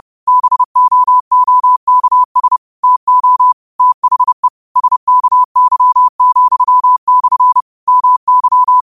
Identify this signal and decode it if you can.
Morse code, heard as a keyed tone, about 30 wpm, 1000 Hz. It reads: BOOKS TO THE SKY,/ MY
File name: qotdfiles / 30wpm